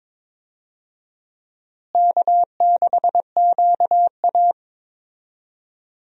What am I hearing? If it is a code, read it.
Morse code, heard as a keyed tone, about 22 wpm, 700 Hz. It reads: K6QA
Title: USA Call Signs - 4 Characters - Repeated 2x - 22wpm